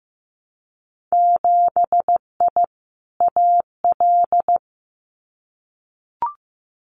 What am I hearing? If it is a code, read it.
Morse code, heard as a keyed tone, about 15 wpm, 700 Hz. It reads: 7I AL
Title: Winter Field Day Exchanges Encore 15wpm